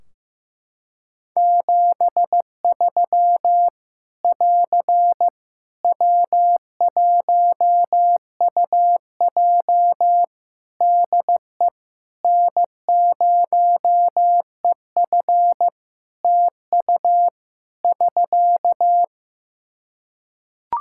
Morse code, heard as a keyed tone, about 15 wpm, 700 Hz. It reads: 73 <AR> W1UJ DE N0EF TU <SK>
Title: Practice QSO - Exchange 4 - Ending - 15wpm